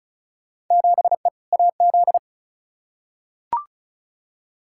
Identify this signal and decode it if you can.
Morse code, heard as a keyed tone, about 34 wpm, 700 Hz. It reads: 7E AZ
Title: Field Day Exchanges Encore - 35wpm